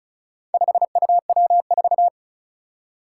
Morse code, heard as a keyed tone, about 36 wpm, 700 Hz. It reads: 5UW4
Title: Four Challenging Letter-Number Combinations 35wpm